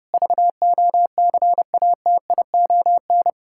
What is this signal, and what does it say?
Morse code, heard as a keyed tone, about 30 wpm, 700 Hz. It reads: VOCATION
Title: Pangram Challenge v2 - 30wpm